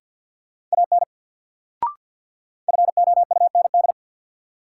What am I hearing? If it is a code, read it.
Morse code, heard as a keyed tone, about 47 wpm, 700 Hz. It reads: AN FOUND
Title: Sentences from Top 300 Words - New Word Drill - 50wpm